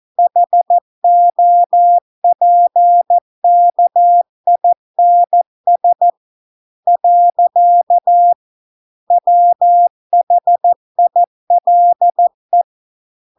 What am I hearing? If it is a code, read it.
Morse code, heard as a keyed tone, about 14 wpm, 700 Hz. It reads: HOPKINS . WHILE